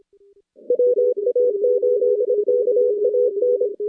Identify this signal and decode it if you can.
Morse code, heard as a keyed tone, about 26 wpm, 490 Hz. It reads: WA8KAN
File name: contests / CW-SS-k5tr- -2100